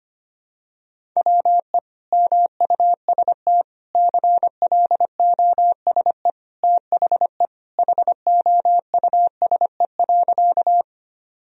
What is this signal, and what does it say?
Morse code, heard as a keyed tone, about 25 wpm, 700 Hz. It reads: WE MUST CLOSE THE HOUSE.
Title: Sentences from Top 1000 Words - Review Part 1 - 25wpm